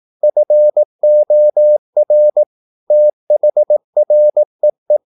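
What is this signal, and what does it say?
Morse code, heard as a keyed tone, about 18 wpm, 600 Hz. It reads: FOR THREE
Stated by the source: Morse practice at 18 wpm, English text, from CW med Gnister.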